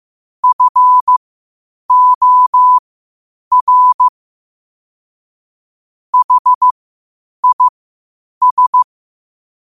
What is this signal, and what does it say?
Morse code, heard as a keyed tone, about 15 wpm, spaced as for 8 wpm, 1000 Hz. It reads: FOR HIS
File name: qotdfiles / 8wpm